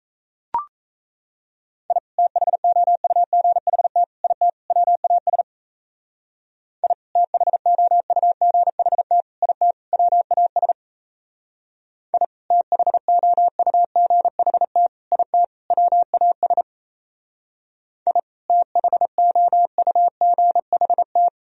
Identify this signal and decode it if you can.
Morse code, heard as a keyed tone, about 35 wpm, 700 Hz. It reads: I THOUGHT IT WAS I THOUGHT IT WAS I THOUGHT IT WAS I THOUGHT